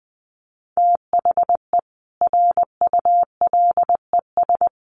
Morse code, heard as a keyed tone, about 20 wpm, 700 Hz. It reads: THE RULES
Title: Sets of 5 Words Encore2 20wpm